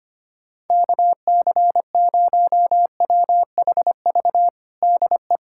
Morse code, heard as a keyed tone, about 25 wpm, 700 Hz. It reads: KC0WHV DE
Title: USA Callsign DE Callsign 25wpm